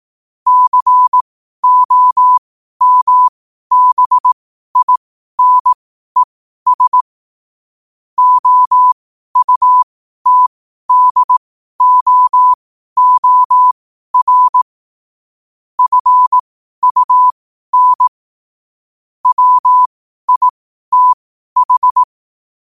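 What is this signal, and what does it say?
Morse code, heard as a keyed tone, about 18 wpm, spaced as for 12 wpm, 1000 Hz. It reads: COMBINES OUTDOOR FUN WITH
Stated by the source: 12 wpm Morse.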